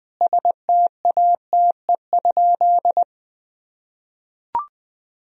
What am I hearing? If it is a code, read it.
Morse code, heard as a keyed tone, about 21 wpm, 700 Hz. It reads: STATE?
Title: Sentences from Top 1000 Words - Review Part 2 - 20wpm